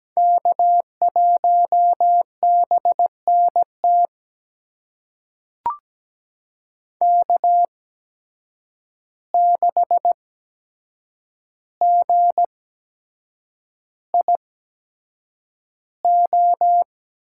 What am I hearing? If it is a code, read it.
Morse code, heard as a keyed tone, about 17 wpm, spaced as for 7 wpm, 700 Hz. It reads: K1BNT K6GIO
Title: USA Call Signs - 5 Characters - 8x Character Spacing - 17wpm